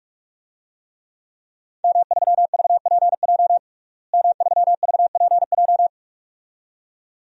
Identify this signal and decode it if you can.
Morse code, heard as a keyed tone, about 46 wpm, 700 Hz. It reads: M3VPJ M3VPJ
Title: DXCC-100 Call Signs - 5 Characters - Repeated 2x - 45wpm